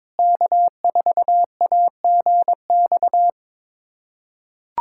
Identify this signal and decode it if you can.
Morse code, heard as a keyed tone, about 22 wpm, 700 Hz. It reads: K4AGX
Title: USA Callsign DE Callsign 22wpm